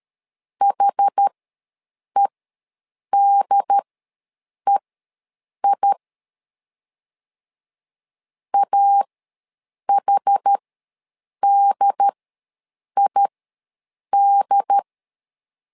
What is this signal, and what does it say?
Morse code, heard as a keyed tone, about 13 wpm, spaced as for 7 wpm, 800 Hz. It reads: HEDEI AHDID
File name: LEZIONI / 2 UHDN / 2 UHD ws 12wpm 1722